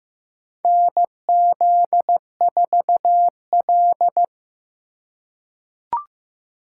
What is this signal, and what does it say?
Morse code, heard as a keyed tone, about 15 wpm, 700 Hz. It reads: NZ4L
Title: USA Call Signs - 4 Characters - 4x Character Spacing - 15wpm